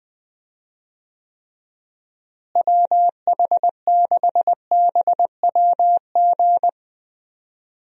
Morse code, heard as a keyed tone, about 20 wpm, 700 Hz. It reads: WH6BWG
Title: USA Call Signs - 6 Characters - Repeated 3x - 20wpm